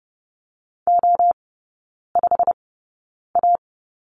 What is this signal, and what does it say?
Morse code, heard as a keyed tone, about 30 wpm, spaced as for 9 wpm, 700 Hz. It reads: O5A
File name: course / 30:10wpm / 087 Lesson 15e - Number 5 - Three Characters 30wpm